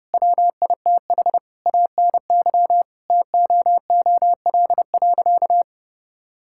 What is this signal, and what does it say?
Morse code, heard as a keyed tone, about 30 wpm, 700 Hz. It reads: WITH ANY TOOL.
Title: Sentences from Top 1000 Words - Review Part 6 - 30wpm